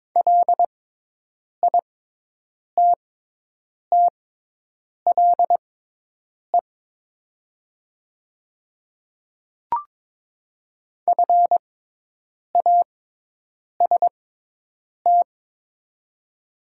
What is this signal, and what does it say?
Morse code, heard as a keyed tone, about 22 wpm, spaced as for 8 wpm, 700 Hz. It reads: LITTLE FAST
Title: 6x Character Spacing - Top 500 Words - 22wpm